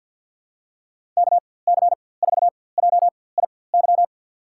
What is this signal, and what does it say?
Morse code, heard as a keyed tone, about 49 wpm, spaced as for 23 wpm, 700 Hz. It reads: KC4JIY
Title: USA Call Signs - 6 Characters - 4x Character Spacing - 50wpm